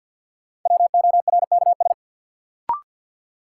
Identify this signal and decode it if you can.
Morse code, heard as a keyed tone, about 52 wpm, 700 Hz. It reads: WORKS
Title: Sentences from Top 1100 Words - New Word Drill Alt - 50wpm